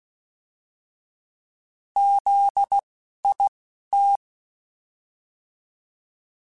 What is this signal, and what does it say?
Morse code, heard as a keyed tone, about 16 wpm, spaced as for 11 wpm, 800 Hz. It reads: ZIT
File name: Alphabetical / Alphabetical mp3s / Zs @10